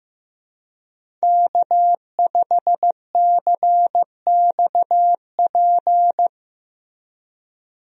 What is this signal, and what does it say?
Morse code, heard as a keyed tone, about 15 wpm, 700 Hz. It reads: K5CXP
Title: USA Call Signs - 5 Characters - Repeated 3x - 15wpm